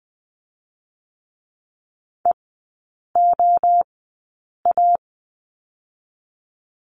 Morse code, heard as a keyed tone, about 20 wpm, 700 Hz. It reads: E O A
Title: 023 Lesson 5e - Letter o - Three Characters 20wpm